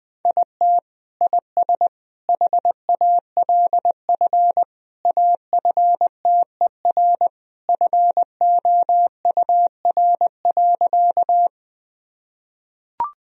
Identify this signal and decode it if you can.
Morse code, heard as a keyed tone, about 20 wpm, 700 Hz. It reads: IT IS HALF AFTER FOUR.
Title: Sentences from Top 1000 Words - Review Part 1 - 20wpm